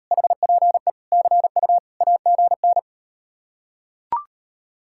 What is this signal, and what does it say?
Morse code, heard as a keyed tone, about 39 wpm, 700 Hz. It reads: HPE CU AGN